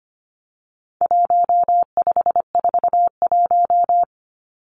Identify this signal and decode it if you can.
Morse code, heard as a keyed tone, about 25 wpm, 700 Hz. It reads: 1541
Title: Numbers - 4-Digit 25wpm